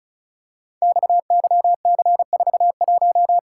Morse code, heard as a keyed tone, about 36 wpm, 690 Hz. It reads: XYC41
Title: Code Groups - 5 Letter:Numbers - 35wpm